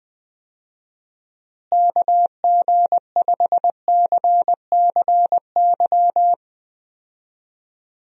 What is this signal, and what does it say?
Morse code, heard as a keyed tone, about 20 wpm, 700 Hz. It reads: KG5CCY
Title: USA Call Signs - 6 Characters - Repeated 2x - 20wpm